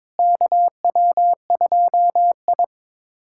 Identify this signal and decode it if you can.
Morse code, heard as a keyed tone, about 22 wpm, 700 Hz. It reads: KW2I
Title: USA Callsign DE Callsign - 4 Characters - 22wpm